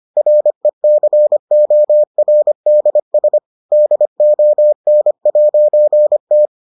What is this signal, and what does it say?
Morse code, heard as a keyed tone, about 25 wpm, 600 Hz. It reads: RECORDS DON'T